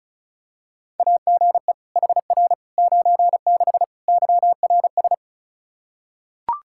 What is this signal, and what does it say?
Morse code, heard as a keyed tone, about 34 wpm, 700 Hz. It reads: AGE HR 96 YRS
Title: K1USN SST QSO Practice 35wpm